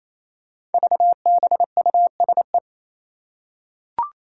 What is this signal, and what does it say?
Morse code, heard as a keyed tone, about 28 wpm, 700 Hz. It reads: VBUSE
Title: Five Challenging Letter-Number Combinations 28wpm